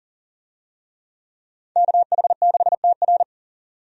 Morse code, heard as a keyed tone, about 40 wpm, 700 Hz. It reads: KH6TR
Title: DXCC-100 Call Signs - 5 Characters - 40wpm